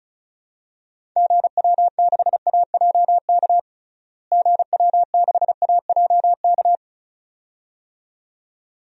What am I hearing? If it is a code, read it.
Morse code, heard as a keyed tone, about 34 wpm, 700 Hz. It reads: GW6AJK GW6AJK
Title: DXCC-100 Call Signs - 6 Characters - Repeated 2x - 35wpm